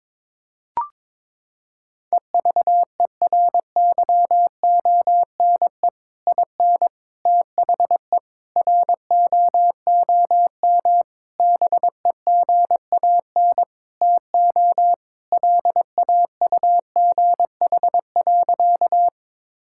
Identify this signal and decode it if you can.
Morse code, heard as a keyed tone, about 22 wpm, 700 Hz. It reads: EVERYONE IN THE ROOM BEGAN TO LAUGH.
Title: Sentences from Top 1000 Words - Review Part 5 - 22wpm